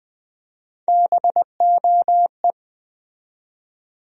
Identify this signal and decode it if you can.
Morse code, heard as a keyed tone, about 21 wpm, 700 Hz. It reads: BOE
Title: USA Call Sign Postfixes - 3 Letters 20wpm